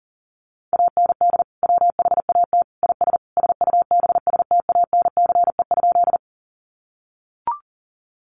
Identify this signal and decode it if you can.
Morse code, heard as a keyed tone, about 40 wpm, 700 Hz. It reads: AND WHAT IS SUBSTANCE?
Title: Sentences from Top 1500 Words 40wpm